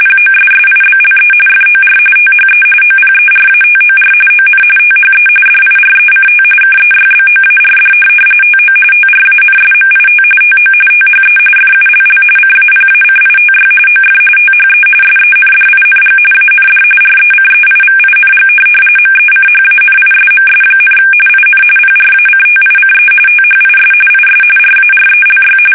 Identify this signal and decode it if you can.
RTTY, about 45.45 baud. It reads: POSSIBLE FOR THE OPERATOR ON THE DIRECT KEYBOARD TO TRANSMIT TO THE FULL CAPACITY OF THE SENDING AND RECEIVING MECHANISMS, IN PRACTICE THIS IS RARELY
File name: fsk850